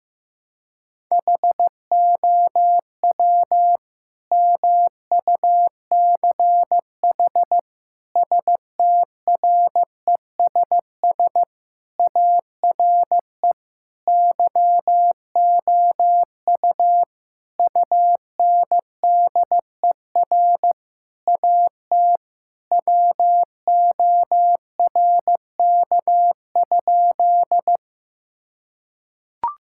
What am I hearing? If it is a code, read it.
Morse code, heard as a keyed tone, about 15 wpm, 700 Hz. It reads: HOW MUCH STRESS ARE YOU UNDER AT WORK?
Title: Sentences from Top 1000 Words - Review Part 6 - 15wpm